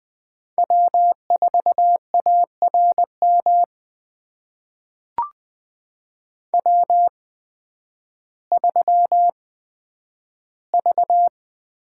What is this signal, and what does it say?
Morse code, heard as a keyed tone, about 20 wpm, spaced as for 11 wpm, 700 Hz. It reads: W4ARM W3V
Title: USA Call Signs - 5 Characters - 8x Character Spacing - 20wpm